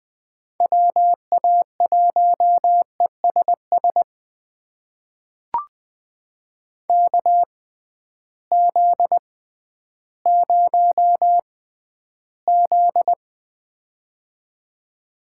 Morse code, heard as a keyed tone, about 20 wpm, spaced as for 12 wpm, 700 Hz. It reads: WA1ESS KZ0Z
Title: USA Call Signs - 6x Character Spacing - 20wpm